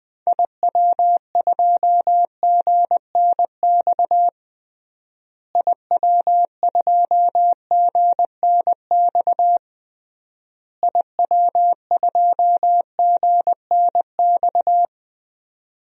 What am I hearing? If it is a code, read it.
Morse code, heard as a keyed tone, about 21 wpm, 700 Hz. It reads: IW2GNX IW2GNX IW2GNX
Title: DXCC-100 Call Signs - 6 Characters - Repeated 3x - 20wpm